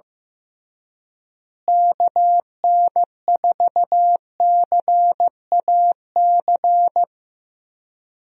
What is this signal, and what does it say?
Morse code, heard as a keyed tone, about 15 wpm, 700 Hz. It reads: KN4CAC
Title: USA Call Signs - 6 Characters - Repeated 2x - 15wpm